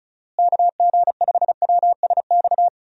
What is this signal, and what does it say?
Morse code, heard as a keyed tone, about 34 wpm, 700 Hz. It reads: KG5WSX
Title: USA Call Signs - 6 Characters - Repeated 3x - 35wpm